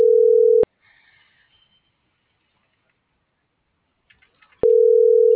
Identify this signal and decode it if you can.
Ringback tone (North America), 440 + 480 Hz.